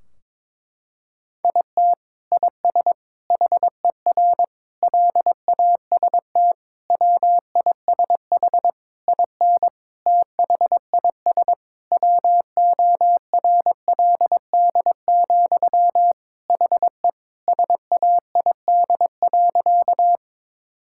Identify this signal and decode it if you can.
Morse code, heard as a keyed tone, about 22 wpm, 700 Hz. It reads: IT IS HER LAST WISH IN THIS WORLD, HE SAID.
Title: Sentences from Top 1600 Words 22wpm